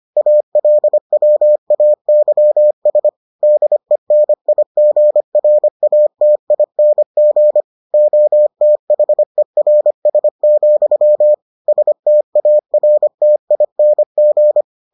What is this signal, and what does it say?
Morse code, heard as a keyed tone, about 25 wpm, 600 Hz. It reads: ALWAYS DENIGRATING OTHERS, STARTING